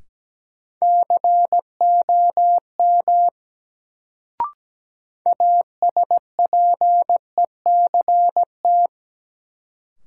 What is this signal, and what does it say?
Morse code, heard as a keyed tone, about 17 wpm, 700 Hz. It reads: COM ASPECT
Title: Sentences from Top 1300 Words - New Word Drill - 17wpm